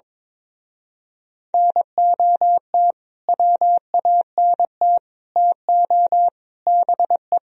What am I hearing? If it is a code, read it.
Morse code, heard as a keyed tone, about 22 wpm, 700 Hz. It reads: NOT WANT TO BE
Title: Speed-Racing - Sets of 4 Words 22wpm